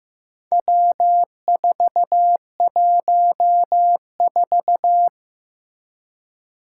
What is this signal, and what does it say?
Morse code, heard as a keyed tone, about 15 wpm, 700 Hz. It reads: W414
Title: Four Challenging Letter-Number Combinations 15wpm